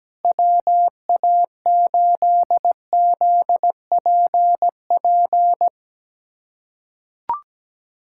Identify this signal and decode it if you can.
Morse code, heard as a keyed tone, about 17 wpm, 700 Hz. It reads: WA8ZPP
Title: USA Callsign DE Callsign - 6 Characters - 17wpm